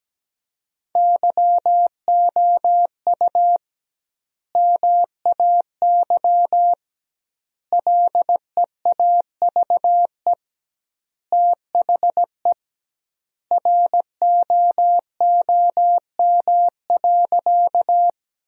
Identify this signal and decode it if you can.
Morse code, heard as a keyed tone, about 17 wpm, 700 Hz. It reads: YOU MAY LEAVE THE ROOM.
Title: 2x Word-Spacing - Sentences from Top 400 Words - 17wpm